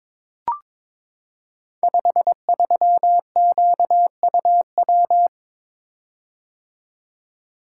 Morse code, heard as a keyed tone, about 22 wpm, 700 Hz. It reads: 53QUW
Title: Code Groups - 5 Letter:Numbers - 22wpm